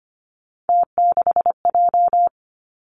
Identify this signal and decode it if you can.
Morse code, heard as a keyed tone, about 25 wpm, 700 Hz. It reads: T6J
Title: Three Challenging Letter-Number Combinations 25wpm